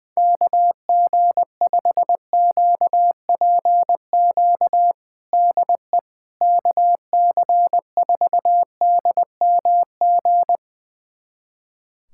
Morse code, heard as a keyed tone, about 20 wpm, 700 Hz. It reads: KG5QPQ DE KC4DMG
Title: USA Callsign DE Callsign - 6 Characters - 20wpm